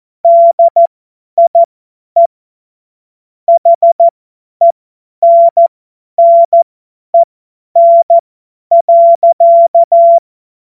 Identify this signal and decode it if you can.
Morse code, heard as a keyed tone, about 14 wpm, spaced as for 10 wpm, 680 Hz. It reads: DIE HENNEN.
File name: Witz-29-Juli-2025-50-F70